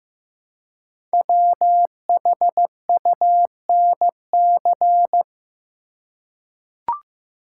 Morse code, heard as a keyed tone, about 15 wpm, 700 Hz. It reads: WHUNC
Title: Code Groups - 5 Letters - 15wpm